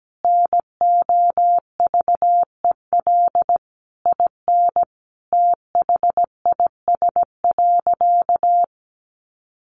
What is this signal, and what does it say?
Morse code, heard as a keyed tone, about 17 wpm, 700 Hz. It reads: NOVEL IN THIS.
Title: Sentences from Top 1300 Words 17wpm